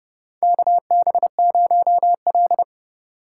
Morse code, heard as a keyed tone, about 30 wpm, 700 Hz. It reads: KB0L
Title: USA Call Signs - 4 Characters - 4x Character Spacing - 30wpm